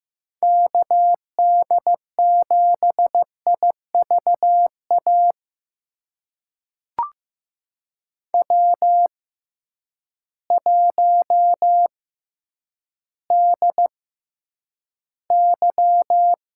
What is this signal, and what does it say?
Morse code, heard as a keyed tone, about 15 wpm, spaced as for 9 wpm, 700 Hz. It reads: KD7IVA W1DY
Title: USA Call Signs - 6x Character Spacing - 15wpm